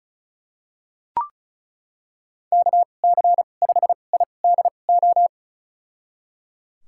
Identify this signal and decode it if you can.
Morse code, heard as a keyed tone, about 34 wpm, spaced as for 25 wpm, 700 Hz. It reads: KC5IDO
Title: USA Call Signs - 2x Character Spacing - 35wpm